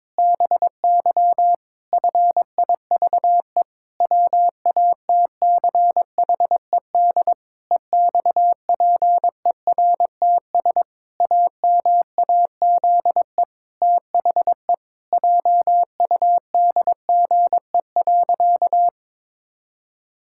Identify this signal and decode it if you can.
Morse code, heard as a keyed tone, about 22 wpm, 700 Hz. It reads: BY FIVE WATCHED EXPERTS AMAZE THE JUDGE.
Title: Pangram Challenge v2 - 22wpm